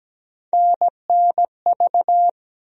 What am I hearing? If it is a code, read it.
Morse code, heard as a keyed tone, about 17 wpm, 700 Hz. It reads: NNV
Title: USA Call Sign Postfixes - 3 Letters 17wpm